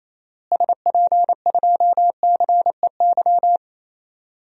Morse code, heard as a keyed tone, about 28 wpm, 700 Hz. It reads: SP2CEY
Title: DXCC-100 Call Signs - 6 Characters - Repeated 3x - 28wpm